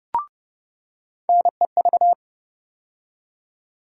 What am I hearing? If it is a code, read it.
Morse code, heard as a keyed tone, about 30 wpm, 700 Hz. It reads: NEV
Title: USA Call Sign Postfixes - 3 Letters 30wpm